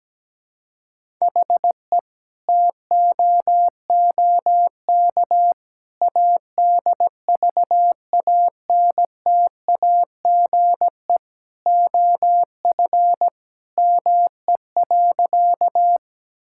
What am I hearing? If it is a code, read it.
Morse code, heard as a keyed tone, about 17 wpm, 700 Hz. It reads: HE TOOK ADVANTAGE OF ME.
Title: Sentences from Top 1000 Words - Review Part 6 - 17wpm